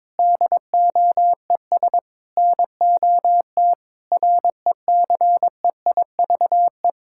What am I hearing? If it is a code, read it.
Morse code, heard as a keyed tone, about 22 wpm, 700 Hz. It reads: DOES NOT RECEIVE
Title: Sentences from Top 1600 Words 22wpm